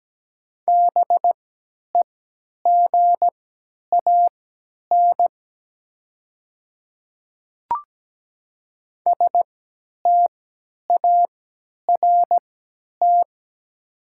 Morse code, heard as a keyed tone, about 17 wpm, spaced as for 10 wpm, 700 Hz. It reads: BEGAN START
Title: 3x Character Spacing - Top 500 Words - 17wpm